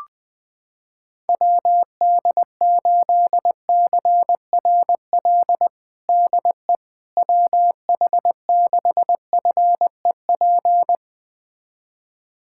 Morse code, heard as a keyed tone, about 20 wpm, 700 Hz. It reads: WD8CRL DE WH6FEP
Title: USA Callsign DE Callsign - 6 Characters - 20wpm